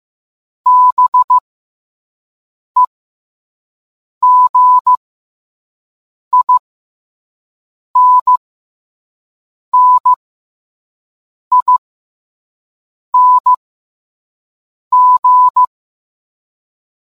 Morse code, heard as a keyed tone, about 15 wpm, spaced as for 5 wpm, 1000 Hz. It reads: BEGINNING